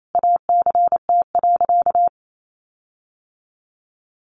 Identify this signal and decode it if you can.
Morse code, heard as a keyed tone, about 28 wpm, 700 Hz. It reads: ACT.
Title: Sentences from Top 1300 Words 28wpm